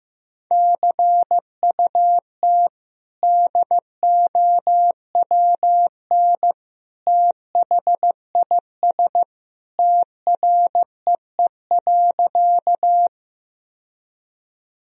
Morse code, heard as a keyed tone, about 15 wpm, 700 Hz. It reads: CUT DOWN THIS TREE.
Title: Sentences from Top 1000 Words - Review Part 1 - 15wpm